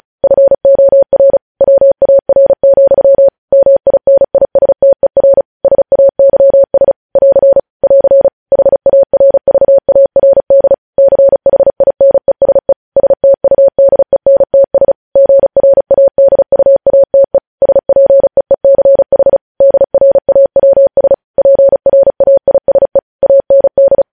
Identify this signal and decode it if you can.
Morse code, heard as a keyed tone, about 36 wpm, 550 Hz. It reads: FOR WAR, MINISTER SAYS <AR> <AR> HARVARD CHINESE STUDENTS GRADUATE SPEECH DRAWS PRAISE AND
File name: podcast / News headlines-1x-35x35